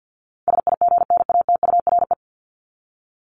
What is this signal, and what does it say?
Morse code, heard as a keyed tone, about 50 wpm, 700 Hz. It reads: SIGNATURE
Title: Top 1751-2000 Words 50wpm